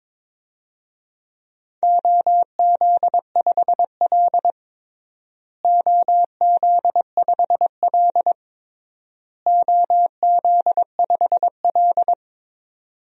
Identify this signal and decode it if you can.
Morse code, heard as a keyed tone, about 22 wpm, 700 Hz. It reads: OZ5L OZ5L OZ5L
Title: DXCC-100 Call Signs - 4 Characters - Repeated 3x - 22wpm